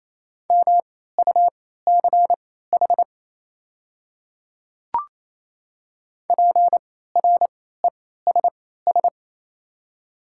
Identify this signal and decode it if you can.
Morse code, heard as a keyed tone, about 28 wpm, spaced as for 16 wpm, 700 Hz. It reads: MUCH PRESS
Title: 3x Character Spacing - Top 300 Words - 28wpm